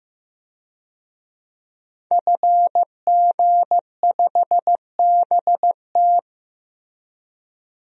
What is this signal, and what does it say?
Morse code, heard as a keyed tone, about 15 wpm, 700 Hz. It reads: FG5BT
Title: DXCC-100 Call Signs - 5 Characters - 15wpm